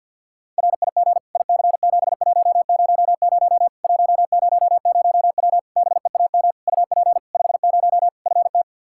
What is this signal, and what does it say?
Morse code, heard as a keyed tone, about 50 wpm, 700 Hz. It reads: RIG IC7100 100W BEAM UP 50 FT
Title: Practice QSO - Exchange 2 - Info - 50wpm